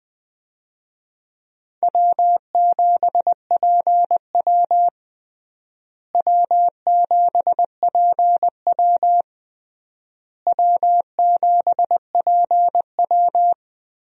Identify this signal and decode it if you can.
Morse code, heard as a keyed tone, about 20 wpm, 700 Hz. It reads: W7PW W7PW W7PW
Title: USA Call Signs - 4 Characters - Repeated 3x - 20wpm